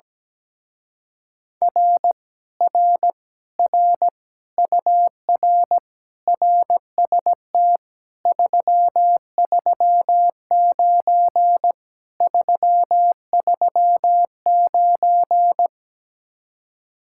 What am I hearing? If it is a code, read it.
Morse code, heard as a keyed tone, about 17 wpm, 700 Hz. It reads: R R R UR RST 339 339
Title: Speed-Racing Common QSO Phrases Encore - 17wpm